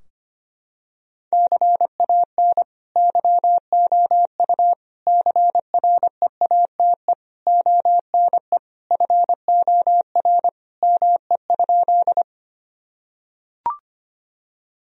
Morse code, heard as a keyed tone, about 25 wpm, 700 Hz. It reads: CAN YOU CREATE ONE FOR ME?
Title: Sentences from Top 1000 Words - Review Part 3 - 25wpm